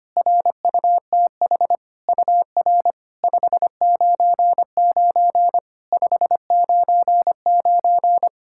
Morse code, heard as a keyed tone, about 25 wpm, 690 Hz. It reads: RUTH UR 599 599
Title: SOTA QSO Practice 25wpm